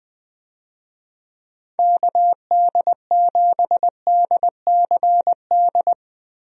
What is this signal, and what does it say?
Morse code, heard as a keyed tone, about 20 wpm, 700 Hz. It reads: KD7DCD
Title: USA Call Signs - 6 Characters - 20wpm